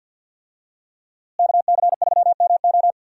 Morse code, heard as a keyed tone, about 51 wpm, 700 Hz. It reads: KC3MO
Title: USA Call Signs - 5 Characters - Repeated 3x - 50wpm